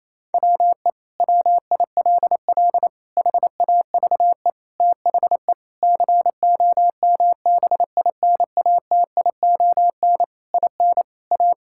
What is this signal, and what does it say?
Morse code, heard as a keyed tone, about 28 wpm, 700 Hz. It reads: WE WILL HAVE THE COMBINATION IN A
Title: Sentences from Top 1000 Words - Review Part 6 - 28wpm